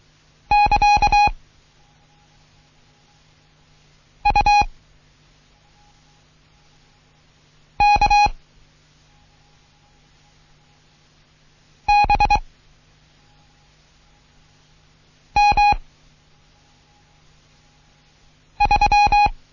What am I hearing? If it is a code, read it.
Morse code, heard as a keyed tone, about 23 wpm, 830 Hz. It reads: <KA> U K B M 3
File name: Morsen / Kurs A 16